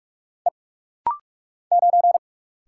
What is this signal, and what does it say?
Morse code, heard as a keyed tone, about 44 wpm, 700 Hz. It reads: E 9
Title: Single Letter-Number Alt - Rapid-Fire 45wpm